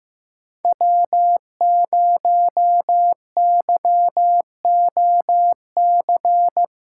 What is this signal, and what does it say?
Morse code, heard as a keyed tone, about 15 wpm, 700 Hz. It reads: W0YOC
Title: USA Call Signs - 5 Characters - 4x Character Spacing - 15wpm